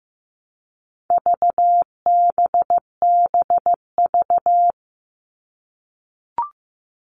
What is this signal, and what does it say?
Morse code, heard as a keyed tone, about 15 wpm, 700 Hz. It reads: VBBV
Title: All Cut Numbers - 4-Digit 15wpm